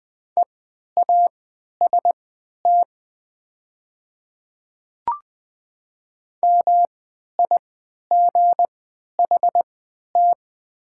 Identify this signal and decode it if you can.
Morse code, heard as a keyed tone, about 20 wpm, spaced as for 11 wpm, 700 Hz. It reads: EAST MIGHT
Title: 3x Character Spacing - Top 500 Words - 20wpm